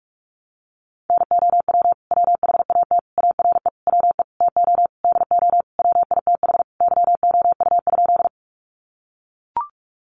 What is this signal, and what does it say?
Morse code, heard as a keyed tone, about 44 wpm, 700 Hz. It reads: NOW WHAT ARE WE TO DO WITH YOU?
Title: Sentences from Top 100 Words 45wpm